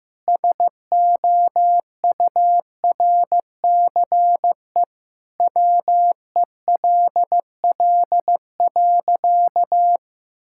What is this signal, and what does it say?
Morse code, heard as a keyed tone, about 15 wpm, 700 Hz. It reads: SOURCE WELL.
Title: Sentences from Top 1000 Words - Review Part 5 - 15wpm